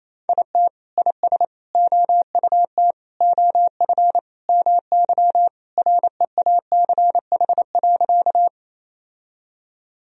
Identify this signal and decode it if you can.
Morse code, heard as a keyed tone, about 28 wpm, 700 Hz. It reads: IT IS OUT OF MY REACH.
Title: Sentences from Top 1000 Words - Review Part 2 - 28wpm